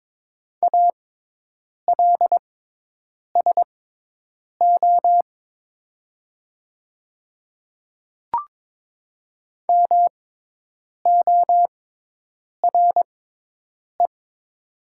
Morse code, heard as a keyed tone, about 22 wpm, spaced as for 8 wpm, 700 Hz. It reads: ALSO MORE
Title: 6x Character Spacing - Top 200 Words - 22wpm